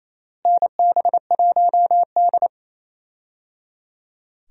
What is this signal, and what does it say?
Morse code, heard as a keyed tone, about 28 wpm, 700 Hz. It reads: NB1D
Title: USA Call Signs - 4 Characters - Repeated 3x - 28wpm